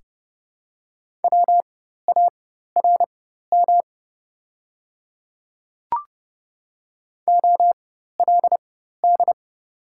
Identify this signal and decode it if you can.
Morse code, heard as a keyed tone, about 30 wpm, spaced as for 14 wpm, 690 Hz. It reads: WARM OLD
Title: 4x Character Spacing - Top 500 Words - 30wpm